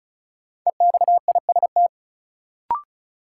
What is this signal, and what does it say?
Morse code, heard as a keyed tone, about 36 wpm, 700 Hz. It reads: EXIST
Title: Sentences from Top 1100 Words - New Word Drill Alt - 35wpm